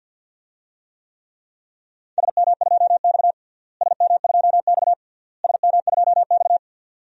Morse code, heard as a keyed tone, about 52 wpm, 700 Hz. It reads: SM2X SM2X SM2X
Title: DXCC-100 Call Signs - 4 Characters - Repeated 3x - 50wpm